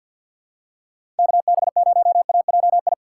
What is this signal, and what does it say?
Morse code, heard as a keyed tone, about 52 wpm, 690 Hz. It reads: KB0AJI
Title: USA Call Signs - 4x Character Spacing - 50wpm